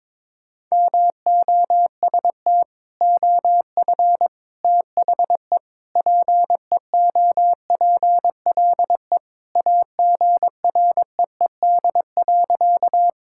Morse code, heard as a keyed tone, about 22 wpm, 700 Hz. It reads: MOST OF THE PEOPLE AGREED.
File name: practice / 22wpm / Sentences with 3-Word Phrases - 22wpm